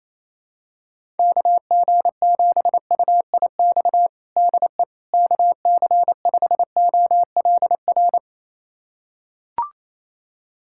Morse code, heard as a keyed tone, about 27 wpm, 700 Hz. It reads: KG7UIX DE KC5OLR
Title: USA Callsign DE Callsign - 6 Characters - 28wpm